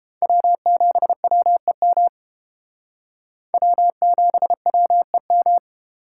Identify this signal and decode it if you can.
Morse code, heard as a keyed tone, about 31 wpm, 700 Hz. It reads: W7WEM W7WEM